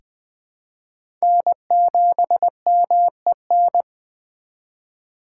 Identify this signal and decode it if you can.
Morse code, heard as a keyed tone, about 20 wpm, 700 Hz. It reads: N7MEN
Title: USA Call Signs - 4x Character Spacing - 20wpm